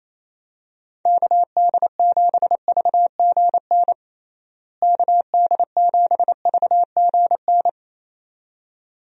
Morse code, heard as a keyed tone, about 28 wpm, 700 Hz. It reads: KD7VGN KD7VGN
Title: USA Call Signs - 6 Characters - Repeated 2x - 28wpm